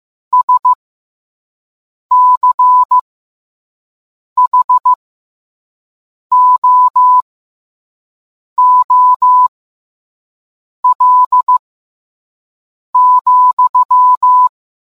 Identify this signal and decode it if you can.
Morse code, heard as a keyed tone, about 15 wpm, spaced as for 5 wpm, 1000 Hz. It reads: SCHOOL,